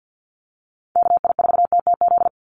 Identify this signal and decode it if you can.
Morse code, heard as a keyed tone, about 50 wpm, 700 Hz. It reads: KI4TTZ
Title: USA Call Signs 50wpm